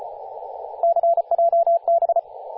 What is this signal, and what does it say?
Morse code, heard as a keyed tone, about 35 wpm, 680 Hz, with radio noise. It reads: CJB
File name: posts / 2016-sep-22-0537z-kh0-kw2x-18-077mhz-cw